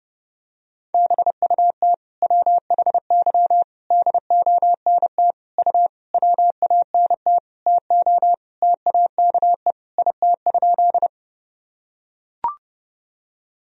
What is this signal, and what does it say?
Morse code, heard as a keyed tone, about 32 wpm, 700 Hz. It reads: BUT WHY DONT U WANT TO TAKE IT?
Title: Sentences from Top 300 Words and Abbreviations 30wpm